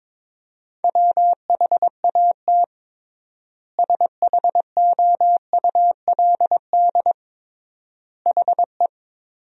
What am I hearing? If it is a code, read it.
Morse code, heard as a keyed tone, about 22 wpm, 700 Hz. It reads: WHAT SHOULD HE
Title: 3x Word-Spacing - Sentences from Top 300 Words - 22wpm